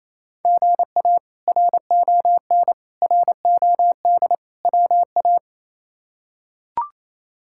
Morse code, heard as a keyed tone, about 28 wpm, 700 Hz. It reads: GA RON ROD WA
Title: K1USN SST QSO Practice - 28wpm